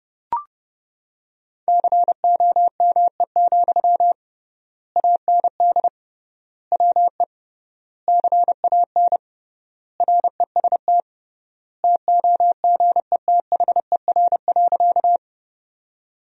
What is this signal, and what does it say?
Morse code, heard as a keyed tone, about 32 wpm, 700 Hz. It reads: COME, AND WE CAN REST TOGETHER.
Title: 3x Word-Spacing - Sentences from Top 500 Words - 30wpm